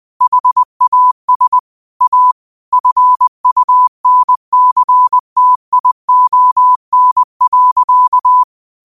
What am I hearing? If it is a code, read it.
Morse code, heard as a keyed tone, about 20 wpm, 1000 Hz. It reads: HAS A FUNCTION.
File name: qotdfiles / 20wpm